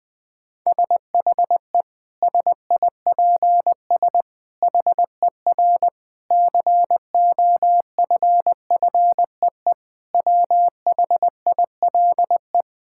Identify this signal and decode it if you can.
Morse code, heard as a keyed tone, about 20 wpm, 700 Hz. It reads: SHE SIPS HER COFFEE WHILE
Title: Difficult Sentences with Hints - 20wpm